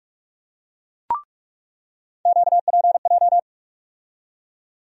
Morse code, heard as a keyed tone, about 44 wpm, 700 Hz. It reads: QPJ
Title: USA Call Sign Postfixes - 3 Letters 45wpm